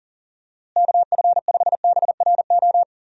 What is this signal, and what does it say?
Morse code, heard as a keyed tone, about 40 wpm, 700 Hz. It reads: KF5BRO
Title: USA Call Signs - 6 Characters - Repeated 3x - 40wpm